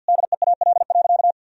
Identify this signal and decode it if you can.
Morse code, heard as a keyed tone, about 50 wpm, 700 Hz. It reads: DEAL.
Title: Sentences from Top 1000 Words - Review Part 4 - 50wpm